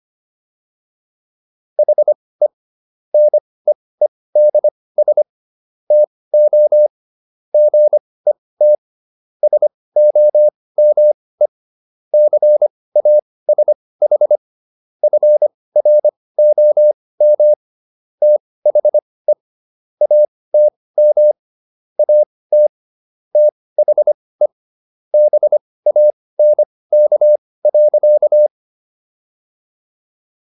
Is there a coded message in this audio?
Yes: Morse code from 1.8 to 28.5 s, heard as a keyed tone, about 25 wpm, spaced as for 18 wpm, 600 Hz. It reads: HE NEEDS TO GET SOME CASH FROM THE ATM AT THE BANK.